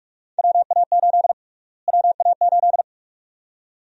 Morse code, heard as a keyed tone, about 45 wpm, 700 Hz. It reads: WA8 WA8
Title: USA Call Sign Prefixes - Repeated 2x - 45wpm